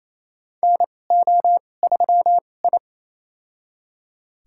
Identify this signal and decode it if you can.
Morse code, heard as a keyed tone, about 28 wpm, spaced as for 20 wpm, 700 Hz. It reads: NO3I
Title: USA Call Signs - 2x Character Spacing - 28wpm